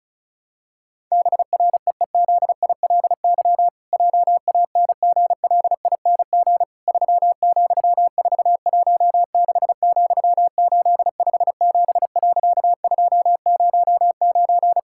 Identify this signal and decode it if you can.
Morse code, heard as a keyed tone, about 36 wpm, 700 Hz. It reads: BREEZILY JANGLING 3,416,857.209
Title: Pangram Challenge v2 - 35wpm